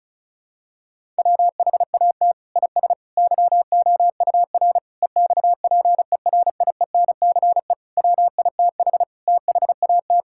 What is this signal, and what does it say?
Morse code, heard as a keyed tone, about 36 wpm, 700 Hz. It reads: WHAT IS YOUR EXPERIENCE WITH THAT
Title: Sentences from Top 1000 Words - Review Part 5 - 35wpm